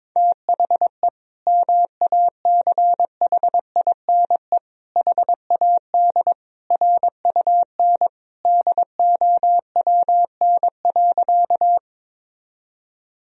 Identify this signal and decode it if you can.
Morse code, heard as a keyed tone, about 22 wpm, 700 Hz. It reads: THE MACHINE HAD RUN DOWN.
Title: Sentences from Top 1000 Words - Review Part 2 - 22wpm